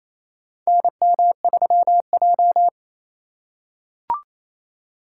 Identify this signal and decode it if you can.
Morse code, heard as a keyed tone, about 28 wpm, 700 Hz. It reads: NM3J
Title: USA Call Signs - 4 Characters - 3x Character Spacing - 28wpm